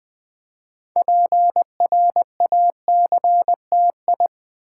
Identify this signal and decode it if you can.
Morse code, heard as a keyed tone, about 20 wpm, 700 Hz. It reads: PRACTI
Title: Build A Word - 2 Letters At A Time - 8-Letter Words - 20wpm